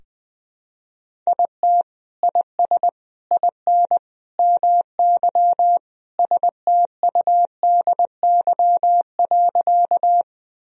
Morse code, heard as a keyed tone, about 20 wpm, 700 Hz. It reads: IT IS IN MY STUDY.
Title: Sentences from Top 1000 Words - Review Part 1 - 20wpm